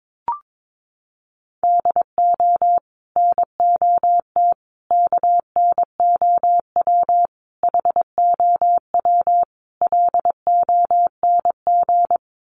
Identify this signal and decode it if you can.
Morse code, heard as a keyed tone, about 22 wpm, 700 Hz. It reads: DO NOT KNOW HOW LONG
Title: Sets of 5 Words Encore2 22wpm